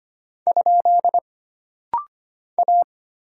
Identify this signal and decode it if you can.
Morse code, heard as a keyed tone, about 25 wpm, 700 Hz. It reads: ? A
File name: Single Character (Letter:Number:Punctuation:Prosign) - Rapid-Fire - 25wpm